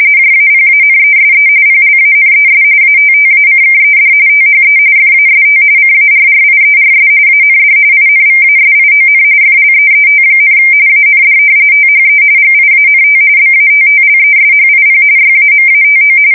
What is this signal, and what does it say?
RTTY, about 45.45 baud. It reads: JAPANESE PLANES ATTACKED THE FAST CARRIERS AND INVASION FLEET, AND THE GIANT BATTLESHIP YAMATO
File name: fsk170 / battleshipnj